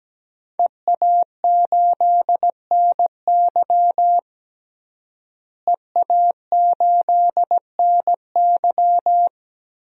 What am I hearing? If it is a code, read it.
Morse code, heard as a keyed tone, about 17 wpm, 700 Hz. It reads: EA8NY EA8NY
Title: DXCC-100 Call Signs - 5 Characters - Repeated 2x - 17wpm